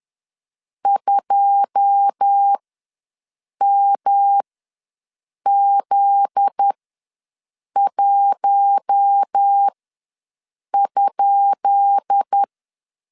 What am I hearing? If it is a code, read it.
Morse code, heard as a keyed tone, about 10 wpm, spaced as for 6 wpm, 800 Hz. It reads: 2MZ1?
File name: LEZIONI / 10 SIGNS / 10D COMA ws 10wpm 4314